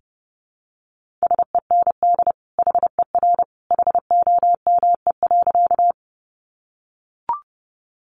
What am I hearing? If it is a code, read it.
Morse code, heard as a keyed tone, about 30 wpm, 700 Hz. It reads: SEND HER HOME.
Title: Sentences from Top 1500 Words 30wpm